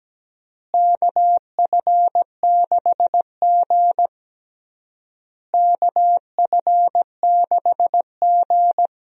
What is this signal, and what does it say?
Morse code, heard as a keyed tone, about 17 wpm, 700 Hz. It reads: KF6G KF6G
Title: USA Call Signs - 4 Characters - Repeated 3x - 17wpm